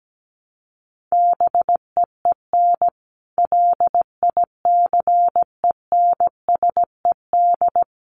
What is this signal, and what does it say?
Morse code, heard as a keyed tone, about 17 wpm, 700 Hz. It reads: BEEN LICENSED
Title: Common QSO Phrases 17wpm